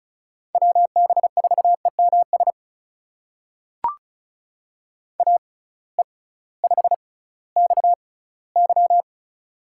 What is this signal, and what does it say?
Morse code, heard as a keyed tone, about 36 wpm, spaced as for 19 wpm, 700 Hz. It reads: WB4EMS AE5XY
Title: USA Call Signs - 6x Character Spacing - 35wpm